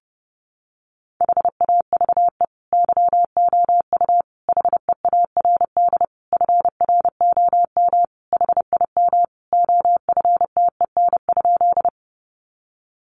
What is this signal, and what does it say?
Morse code, heard as a keyed tone, about 30 wpm, 700 Hz. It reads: HAVE YOU HEARD FROM HIM OFTEN?
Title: Sentences from Top 400 Words 30wpm